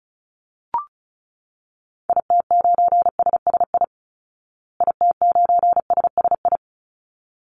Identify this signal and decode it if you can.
Morse code, heard as a keyed tone, about 34 wpm, 700 Hz. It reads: IT9SSI IT9SSI
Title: Call Signs Repeated 3x - 35wpm